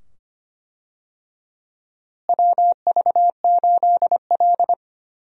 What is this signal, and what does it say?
Morse code, heard as a keyed tone, about 25 wpm, 700 Hz. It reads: WV8L
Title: USA Call Signs - 4 Characters - Repeated 2x - 25wpm